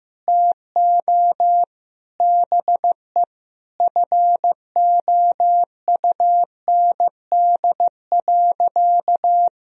Morse code, heard as a keyed tone, about 15 wpm, 700 Hz. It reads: TO BE FOUND.
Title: Sentences from Top 1000 Words - Review Part 3 - 15wpm